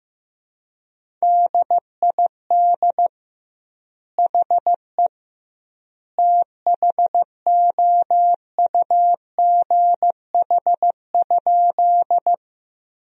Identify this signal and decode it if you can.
Morse code, heard as a keyed tone, about 15 wpm, 700 Hz. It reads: DID HE THOUGH?
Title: 2x Word-Spacing - Sentences from Top 400 Words - 15wpm